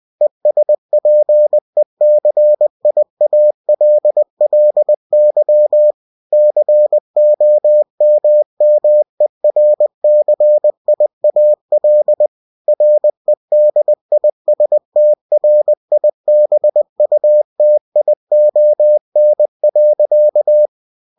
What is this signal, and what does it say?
Morse code, heard as a keyed tone, about 20 wpm, 600 Hz. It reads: ESPECIALLY COMMERCIAL REDISTRIBUTION.
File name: cw-books / winnie-the-pooh / winnie-20wpm-page-0112